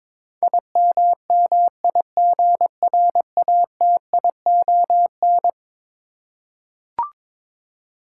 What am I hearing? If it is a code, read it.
Morse code, heard as a keyed tone, about 22 wpm, 700 Hz. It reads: IMMIGRATION
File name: practice / 22wpm / Eleven Letter Words - 22wpm